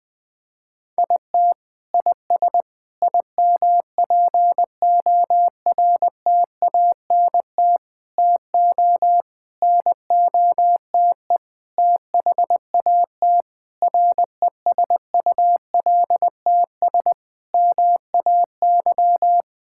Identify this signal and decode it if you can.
Morse code, heard as a keyed tone, about 20 wpm, 700 Hz. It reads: IT IS IMPORTANT TO NOTE THAT RESULTS MAY
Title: Sentences with 6-Word Phrases - Spoken Beginning - 20wpm